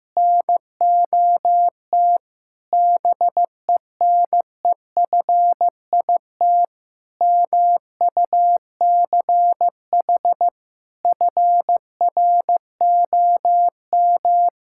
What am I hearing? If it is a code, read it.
Morse code, heard as a keyed tone, about 15 wpm, 700 Hz. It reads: NOT BENEFIT MUCH FROM
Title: Sentences from Top 1000 Words - Review Part 6 - 15wpm